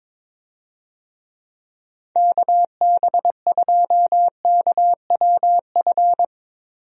Morse code, heard as a keyed tone, about 22 wpm, 700 Hz. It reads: KB2KWF
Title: USA Call Signs - 6 Characters - Repeated 2x - 22wpm